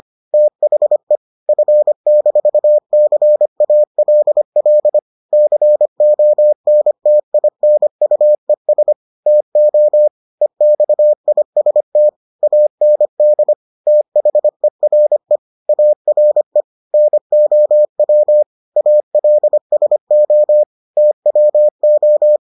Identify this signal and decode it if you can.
Morse code, heard as a keyed tone, about 25 wpm, 600 Hz. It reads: THE F-CALL CONTINUES TO EXIST AND THERE ARE NOW ALSO TWO